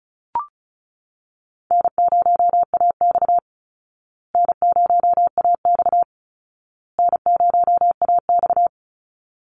Morse code, heard as a keyed tone, about 34 wpm, 700 Hz. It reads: N0AX N0AX N0AX
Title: Call Signs Repeated 2x - 35wpm